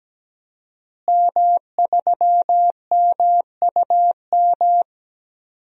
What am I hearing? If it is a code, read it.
Morse code, heard as a keyed tone, about 17 wpm, 700 Hz. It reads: M3MUM
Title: DXCC-100 Call Signs - 5 Characters - Repeated 2x - 17wpm